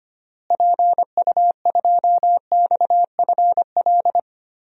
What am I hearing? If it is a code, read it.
Morse code, heard as a keyed tone, about 25 wpm, 700 Hz. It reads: PU2XFL
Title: DXCC-100 Call Signs - 6 Characters - Repeated 2x - 25wpm